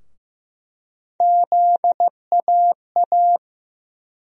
Morse code, heard as a keyed tone, about 15 wpm, 700 Hz. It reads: ZAA
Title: USA Call Sign Postfixes - 3 Letters 15wpm